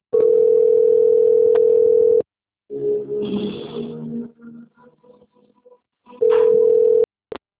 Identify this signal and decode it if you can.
Ringback tone (North America), 440 + 480 Hz.
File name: out- - - - - .21803